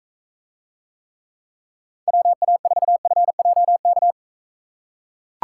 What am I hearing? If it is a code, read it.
Morse code, heard as a keyed tone, about 42 wpm, 700 Hz. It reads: WA4FJK